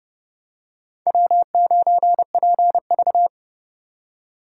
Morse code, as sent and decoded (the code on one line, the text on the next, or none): .-- ----. .--. ...-
W9PV